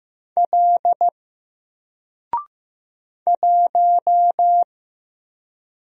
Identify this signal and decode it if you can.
Morse code, heard as a keyed tone, about 15 wpm, 700 Hz. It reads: L 1
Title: Single Letter-Number 15wpm